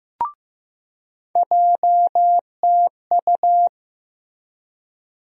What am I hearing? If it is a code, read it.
Morse code, heard as a keyed tone, about 15 wpm, 700 Hz. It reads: JTU
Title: USA Call Sign Postfixes - 3 Letters 15wpm